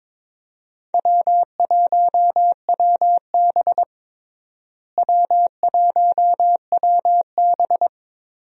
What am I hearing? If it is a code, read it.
Morse code, heard as a keyed tone, about 22 wpm, 700 Hz. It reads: W1WB W1WB
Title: USA Call Signs - 4 Characters - Repeated 3x - 22wpm